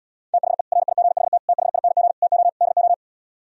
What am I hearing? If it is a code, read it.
Morse code, heard as a keyed tone, about 77 wpm, 700 Hz. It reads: THE RIGHT THING TO DO